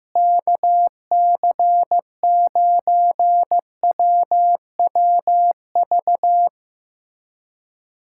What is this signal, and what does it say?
Morse code, heard as a keyed tone, about 15 wpm, 700 Hz. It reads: KC9WWV
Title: USA Callsign DE Callsign 15wpm